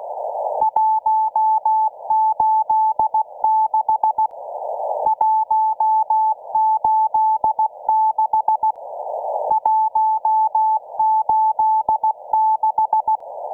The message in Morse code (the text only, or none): 186 186 186